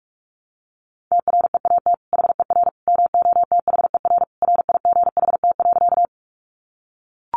Morse code, heard as a keyed tone, about 44 wpm, 700 Hz. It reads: TREAT HER MOTHER RIGHT.
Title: Sentences from Top 1300 Words 45wpm